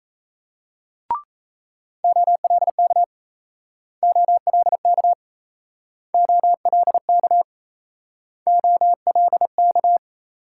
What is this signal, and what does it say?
Morse code, heard as a keyed tone, about 37 wpm, 700 Hz. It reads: OLK OLK OLK OLK